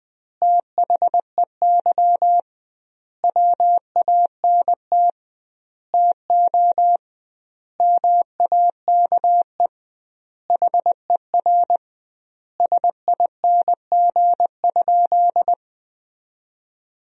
Morse code, heard as a keyed tone, about 20 wpm, 700 Hz. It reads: THEY WANT TO MAKE HER SING?
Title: 2x Word-Spacing - Sentences from Top 400 Words - 20wpm